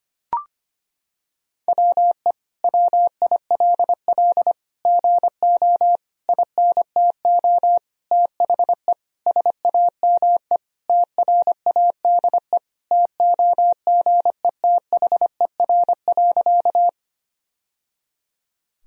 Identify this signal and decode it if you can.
Morse code, heard as a keyed tone, about 25 wpm, 700 Hz. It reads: WE WILL GO INTO THE SAME TRADE TOGETHER.
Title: Sentences from Top 1000 Words - Review Part 4 - 25wpm